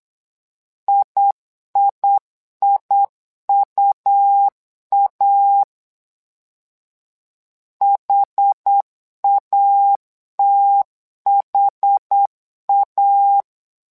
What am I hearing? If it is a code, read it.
Morse code, heard as a keyed tone, about 8 wpm, 800 Hz. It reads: IIIUA HATHA
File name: LEZIONI / 2 UHDN / 2 UHD 8wpm 2258